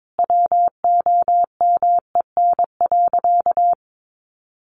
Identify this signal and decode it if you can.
Morse code, heard as a keyed tone, about 22 wpm, 700 Hz. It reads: WOMEN.
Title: Sentences from Top 1300 Words 22wpm